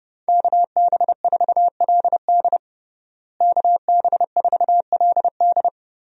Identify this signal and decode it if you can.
Morse code, heard as a keyed tone, about 32 wpm, 700 Hz. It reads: KB4LD KB4LD
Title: USA Call Signs - 5 Characters - Repeated 2x - 30wpm